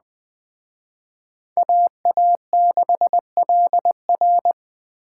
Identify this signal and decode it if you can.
Morse code, heard as a keyed tone, about 20 wpm, 700 Hz. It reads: AA6LR
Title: Speed Racing - USA 2x2 Call Signs 17wpm